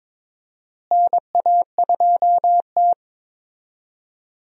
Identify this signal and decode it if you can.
Morse code, heard as a keyed tone, about 22 wpm, 700 Hz. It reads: NA2T
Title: USA Call Signs - 4 Characters - Repeated 2x - 22wpm